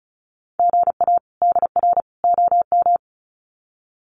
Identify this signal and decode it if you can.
Morse code, heard as a keyed tone, about 34 wpm, 700 Hz. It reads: GA DR OM
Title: Common QSO Phrases 35wpm